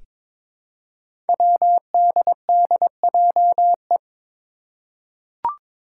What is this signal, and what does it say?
Morse code, heard as a keyed tone, about 22 wpm, 700 Hz. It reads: WDDJE
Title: Five Challenging Letter-Number Combinations 22wpm